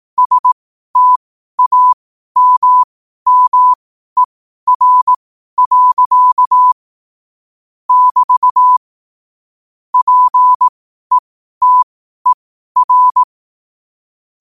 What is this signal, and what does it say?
Morse code, heard as a keyed tone, about 18 wpm, spaced as for 12 wpm, 1000 Hz. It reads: STAMMER. <BT> PETER
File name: 12wpm